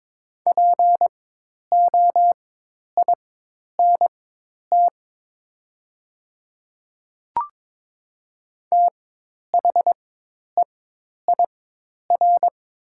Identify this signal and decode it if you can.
Morse code, heard as a keyed tone, about 22 wpm, spaced as for 10 wpm, 700 Hz. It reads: POINT THEIR
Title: 4x Character Spacing - Top 500 Words - 22wpm